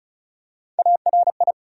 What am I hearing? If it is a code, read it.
Morse code, heard as a keyed tone, about 36 wpm, 700 Hz. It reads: ARI
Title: USA Call Sign Postfixes - 3 Letters 35wpm